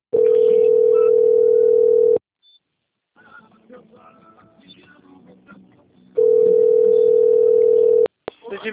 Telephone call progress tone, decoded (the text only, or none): ringback tone